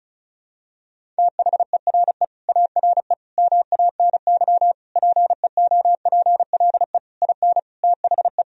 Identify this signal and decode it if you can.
Morse code, heard as a keyed tone, about 36 wpm, 700 Hz. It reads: THERE ARE MANY PEOPLE IN THE
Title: Sentences from Top 1000 Words - Review Part 1 - 35wpm